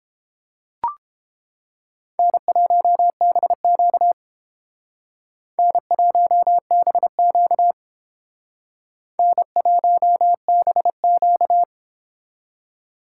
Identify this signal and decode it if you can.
Morse code, heard as a keyed tone, about 29 wpm, 700 Hz. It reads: N1BQ N1BQ N1BQ